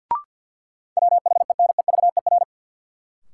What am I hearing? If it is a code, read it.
Morse code, heard as a keyed tone, about 52 wpm, 700 Hz. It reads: WHENEVER
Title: Sentences from Top 1200 Words - New Word Drill - 50wpm